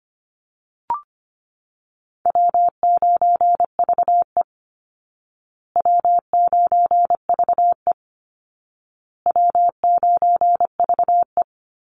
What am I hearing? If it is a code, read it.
Morse code, heard as a keyed tone, about 25 wpm, 700 Hz. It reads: W9VE W9VE W9VE
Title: Call Signs Repeated 2x - 25wpm